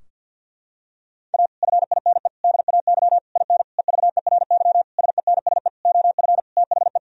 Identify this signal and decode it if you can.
Morse code, heard as a keyed tone, about 51 wpm, 700 Hz. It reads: A FINE DAY IN EVERY SENSE OF THE
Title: Sentences from Top 1000 Words - Review Part 4 - 50wpm